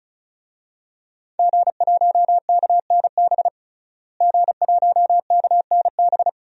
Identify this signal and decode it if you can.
Morse code, heard as a keyed tone, about 36 wpm, 700 Hz. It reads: G1KNB G1KNB
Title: DXCC-100 Call Signs - 5 Characters - Repeated 2x - 35wpm